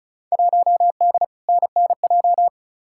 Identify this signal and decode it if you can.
Morse code, heard as a keyed tone, about 36 wpm, 700 Hz. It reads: 1D NNJ
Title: Field Day Exchanges - 35wpm